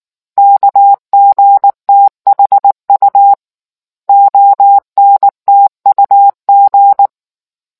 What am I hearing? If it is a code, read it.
Morse code, heard as a keyed tone, about 19 wpm, 800 Hz. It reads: KGTHU ONTUG